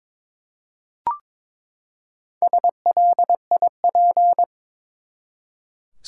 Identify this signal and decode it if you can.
Morse code, heard as a keyed tone, about 21 wpm, 700 Hz. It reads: SLIP
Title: Four Letter Words - 22wpm